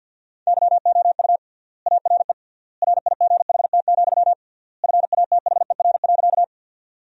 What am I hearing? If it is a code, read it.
Morse code, heard as a keyed tone, about 53 wpm, 700 Hz. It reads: YOU ARE RIGHT, FATHER.
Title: 3x Word-Spacing - Sentences from Top 300 Words - 50wpm